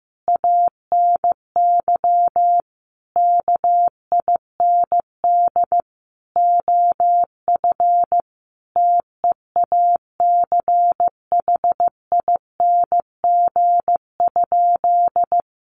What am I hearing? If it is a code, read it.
Morse code, heard as a keyed tone, about 15 wpm, 700 Hz. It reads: ANY KIND OF TEACHING?
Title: Sentences from Top 1100 Words 15wpm